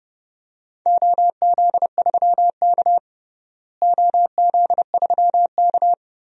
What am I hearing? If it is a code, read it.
Morse code, heard as a keyed tone, about 30 wpm, 700 Hz. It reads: OZ3K OZ3K
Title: DXCC-100 Call Signs - 4 Characters - Repeated 2x - 30wpm